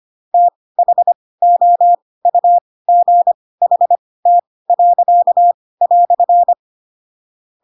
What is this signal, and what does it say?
Morse code, heard as a keyed tone, about 25 wpm, spaced as for 18 wpm, 700 Hz. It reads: THOUGHT."
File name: Pinocchio ch3 12 25 ik1hge